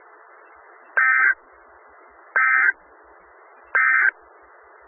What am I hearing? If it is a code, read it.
Emergency Alert System (SAME) tones, heard as harsh buzzing bursts. It is heard from 0.9 to 4.1 s.
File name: Skywarn/2013/2013-03-18-Newton County WA4ASI Repeater